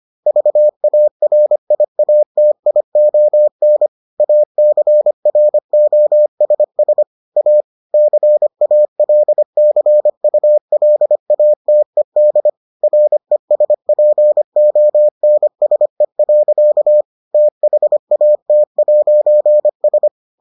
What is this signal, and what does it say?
Morse code, heard as a keyed tone, about 25 wpm, 600 Hz. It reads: VARIATION ACROSS A CALCULATED RESPONSE. THAT'S